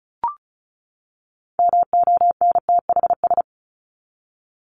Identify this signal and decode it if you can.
Morse code, heard as a keyed tone, about 34 wpm, 700 Hz. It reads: MONTHS
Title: ES:S Suffix - Top 100 Words with Base Words 35wpm